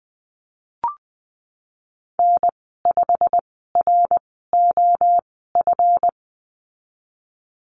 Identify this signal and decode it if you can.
Morse code, heard as a keyed tone, about 20 wpm, spaced as for 15 wpm, 700 Hz. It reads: N5ROF
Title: 161 Lesson 26g - Letter g - Call Signs 20wpm